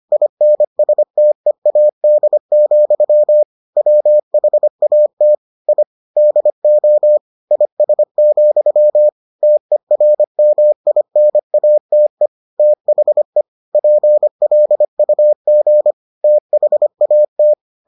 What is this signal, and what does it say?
Morse code, heard as a keyed tone, about 25 wpm, 600 Hz. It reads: INSTEAD, WHAT I DO IS, TERMINATE THE PLUG THAT